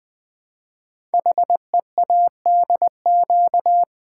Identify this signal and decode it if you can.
Morse code, heard as a keyed tone, about 21 wpm, 700 Hz. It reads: HEADQ
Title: Build A Word - 5 Letters At A Time - 12-Letter Words - 20wpm